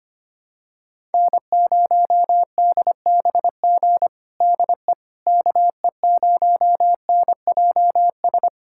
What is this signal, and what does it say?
Morse code, heard as a keyed tone, about 25 wpm, 700 Hz. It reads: N0DBG DE KE0NJS
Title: USA Callsign DE Callsign 25wpm